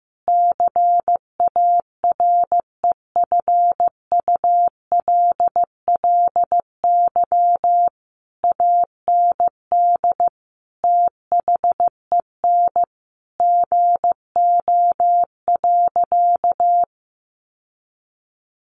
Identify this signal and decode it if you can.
Morse code, heard as a keyed tone, about 14 wpm, 700 Hz. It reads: CAREFULLY AND THEN GO.
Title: Sentences from Top 1300 Words 15wpm